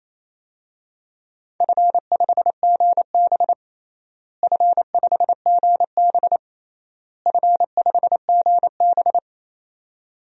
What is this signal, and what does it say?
Morse code, heard as a keyed tone, about 28 wpm, 700 Hz. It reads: F5GB F5GB F5GB
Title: DXCC-100 Call Signs - 4 Characters - Repeated 3x - 28wpm